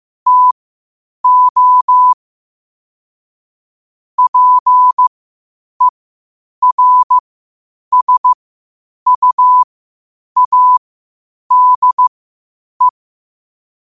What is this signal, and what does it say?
Morse code, heard as a keyed tone, about 15 wpm, spaced as for 8 wpm, 1000 Hz. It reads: TO PERSUADE
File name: qotdfiles / 8wpm